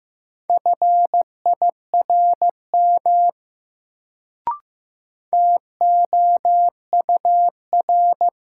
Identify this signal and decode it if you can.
Morse code, heard as a keyed tone, about 15 wpm, 700 Hz. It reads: FIRM TOUR
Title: Sentences from Top 1400 Words - New Word Drill - 15wpm